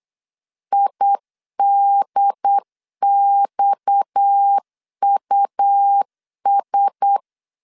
Morse code, heard as a keyed tone, about 8 wpm, 800 Hz. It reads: IDXUS
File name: CW / DET / 1 6 1500C S B 8wpm 5142